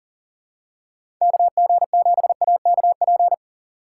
Morse code, heard as a keyed tone, about 40 wpm, 700 Hz. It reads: KG7AKP
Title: USA Call Signs - 3x Character Spacing - 40wpm